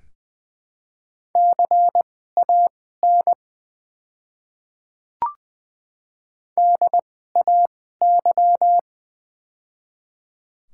Morse code, heard as a keyed tone, about 20 wpm, spaced as for 15 wpm, 690 Hz. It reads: CAN DAY
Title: 2x Character Spacing - Top 200 Words - 20wpm